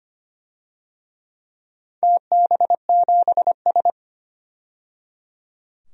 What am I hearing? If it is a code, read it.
Morse code, heard as a keyed tone, about 25 wpm, 700 Hz. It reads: TB7S
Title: DXCC-100 Call Signs - 4 Characters - 25wpm